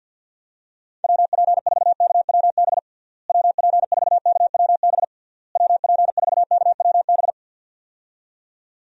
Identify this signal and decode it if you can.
Morse code, heard as a keyed tone, about 47 wpm, 700 Hz. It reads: WP4KWB WP4KWB WP4KWB
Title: USA Call Signs - 6 Characters - Repeated 3x - 50wpm